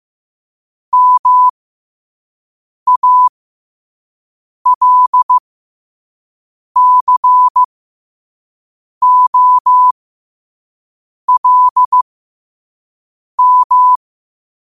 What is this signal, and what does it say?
Morse code, heard as a keyed tone, about 15 wpm, spaced as for 5 wpm, 1000 Hz. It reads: MALCOLM
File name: qotdfiles / 5wpm